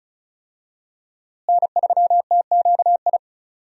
Morse code, heard as a keyed tone, about 36 wpm, 700 Hz. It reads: N3TQI
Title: USA Call Signs - 5 Characters - Repeated 2x - 35wpm